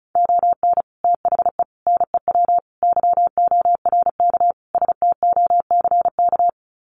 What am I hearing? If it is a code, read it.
Morse code, heard as a keyed tone, about 34 wpm, 700 Hz. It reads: ON THE NEW YORK STOCK
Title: Sets of 5 Words Encore 35wpm